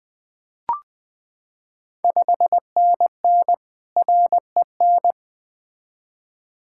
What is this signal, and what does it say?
Morse code, heard as a keyed tone, about 20 wpm, 700 Hz. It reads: 5NN REN
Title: Ontario QSO Party Exchange 20wpm